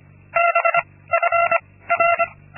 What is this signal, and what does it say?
Morse code, heard as a keyed tone, about 25 wpm, 680 Hz. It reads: B F R
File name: Linksmakalnis-2